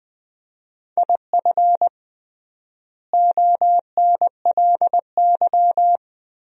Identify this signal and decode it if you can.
Morse code, heard as a keyed tone, about 20 wpm, 700 Hz. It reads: IF ONLY
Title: 3x Word-Spacing - Sentences from Top 500 Words - 20wpm